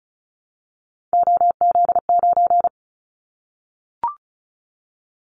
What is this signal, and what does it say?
Morse code, heard as a keyed tone, about 36 wpm, 690 Hz. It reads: OZ9
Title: Callsign Prefixes for Top 50 DXCC - 35wpm